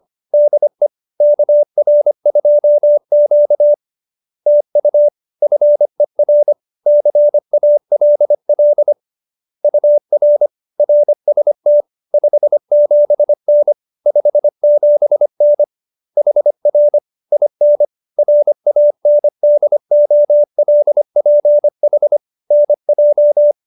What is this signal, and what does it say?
Morse code, heard as a keyed tone, about 25 wpm, 590 Hz. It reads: DE KR2Q TU FER CALL UR RST 57N 57N HR IN RANDOLPH NJ